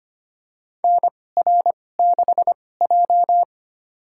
Morse code, heard as a keyed tone, about 25 wpm, spaced as for 18 wpm, 700 Hz. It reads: NR6J
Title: USA Call Signs - 4 Characters - 2x Character Spacing - 25wpm